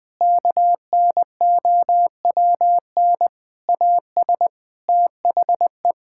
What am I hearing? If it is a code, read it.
Morse code, heard as a keyed tone, about 20 wpm, 700 Hz. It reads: KNOWN AS THE
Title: Sentences with 8-Word Phrases - 20wpm